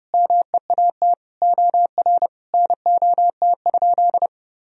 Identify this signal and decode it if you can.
Morse code, heard as a keyed tone, about 30 wpm, 700 Hz. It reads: MEAT OR NOT?
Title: Sentences from Top 1000 Words - Review Part 3 - 30wpm